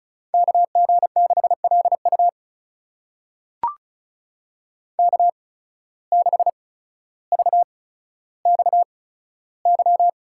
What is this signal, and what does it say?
Morse code, heard as a keyed tone, about 36 wpm, spaced as for 15 wpm, 700 Hz. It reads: KG6LU K6VXY
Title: USA Call Signs - 5 Characters - 8x Character Spacing - 35wpm